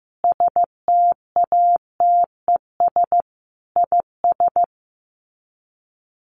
Morse code, heard as a keyed tone, about 15 wpm, 700 Hz. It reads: STATES IS
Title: Sets of 5 Words Encore 15wpm